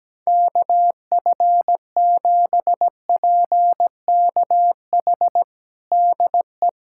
Morse code, heard as a keyed tone, about 17 wpm, 700 Hz. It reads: KF7PKH DE
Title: USA Callsign DE Callsign 17wpm